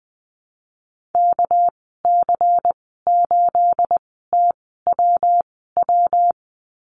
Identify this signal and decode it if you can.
Morse code, heard as a keyed tone, about 20 wpm, spaced as for 14 wpm, 700 Hz. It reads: KC8TWW
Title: 238 Lesson 38g - Letter q - Call Signs 20wpm